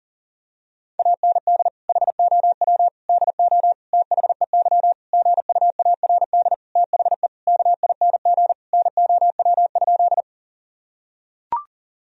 Morse code, heard as a keyed tone, about 40 wpm, 700 Hz. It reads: AND HOW DO THEY GUARD THE KING NOW?
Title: Sentences from Top 1000 Words - Review Part 4 - 40wpm